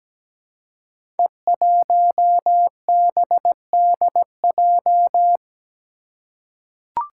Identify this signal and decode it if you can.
Morse code, heard as a keyed tone, about 17 wpm, 700 Hz. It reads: E1BDJ
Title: Code Groups - 5 Letter:Numbers - 17wpm